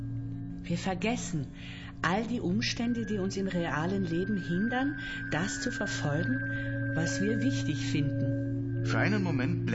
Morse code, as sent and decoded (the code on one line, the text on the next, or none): none